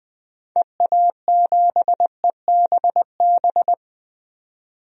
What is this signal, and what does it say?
Morse code, heard as a keyed tone, about 21 wpm, 700 Hz. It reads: EA7EBB
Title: DXCC-100 Call Signs - 6 Characters - 20wpm